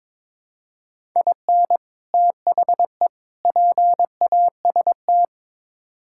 Morse code, heard as a keyed tone, about 22 wpm, 700 Hz. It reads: IN THE PAST
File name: practice / 22wpm / Build A Phrase - 5-Word Phrases - 22wpm